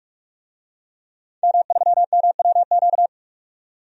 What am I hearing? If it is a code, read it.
Morse code, heard as a keyed tone, about 46 wpm, 700 Hz. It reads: M3MWQ
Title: DXCC-100 Call Signs - 5 Characters - 45wpm